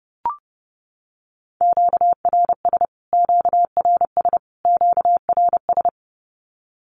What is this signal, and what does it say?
Morse code, heard as a keyed tone, about 30 wpm, 700 Hz. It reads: QRS QRS QRS
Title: Q-Codes Repeated 3x 30wpm